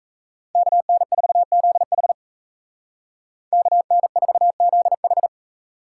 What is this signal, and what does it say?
Morse code, heard as a keyed tone, about 34 wpm, 700 Hz. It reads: KN4ZH KN4ZH
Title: Speed Racing - USA 2x2 Call Signs 28wpm